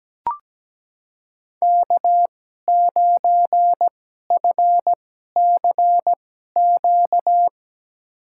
Morse code, heard as a keyed tone, about 17 wpm, spaced as for 12 wpm, 700 Hz. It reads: K9FCQ
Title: USA Call Signs - 5 Characters - 2x Character Spacing - 17wpm